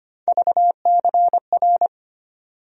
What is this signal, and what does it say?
Morse code, heard as a keyed tone, about 25 wpm, 700 Hz. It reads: VCR
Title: USA Call Sign Postfixes - 3 Letters 25wpm